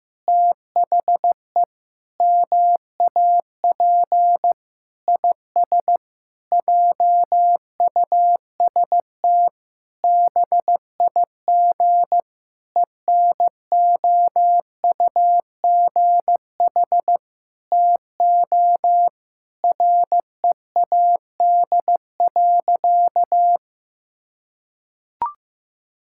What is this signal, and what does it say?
Morse code, heard as a keyed tone, about 15 wpm, 700 Hz. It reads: THE MAP IS JUST BIG ENOUGH TO READ.
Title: Sentences from Top 1000 Words - Review Part 5 - 15wpm